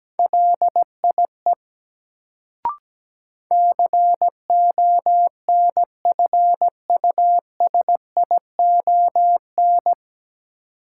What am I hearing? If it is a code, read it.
Morse code, heard as a keyed tone, about 17 wpm, 700 Hz. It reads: LIE CONFUSION
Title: Sentences from Top 1600 Words - New Word Drill Alt - 17wpm